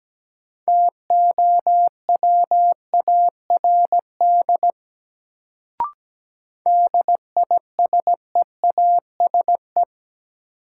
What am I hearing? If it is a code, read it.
Morse code, heard as a keyed tone, about 17 wpm, 700 Hz. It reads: TOWARD DISEASE
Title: Sentences from Top 1000 Words - Review Part 6 - New Word Drill - 17wpm